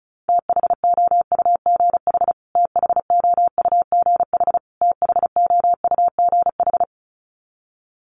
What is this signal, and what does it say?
Morse code, heard as a keyed tone, about 34 wpm, 700 Hz. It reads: THOUGH THOUGH THOUGH
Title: Top 500 Words Repeated 35wpm